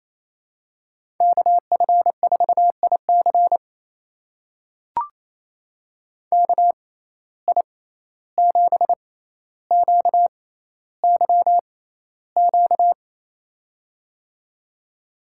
Morse code, heard as a keyed tone, about 28 wpm, spaced as for 14 wpm, 700 Hz. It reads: KF4IC KI7QYQ
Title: USA Call Signs - 6x Character Spacing - 28wpm